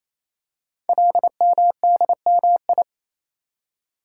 Morse code, heard as a keyed tone, about 28 wpm, 700 Hz. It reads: LMDMI
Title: Code Groups - 5 Letters - 28wpm